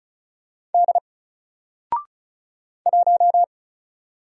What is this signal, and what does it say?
Morse code, heard as a keyed tone, about 36 wpm, 700 Hz. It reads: D 1
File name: Single-Letter-Number-35wpm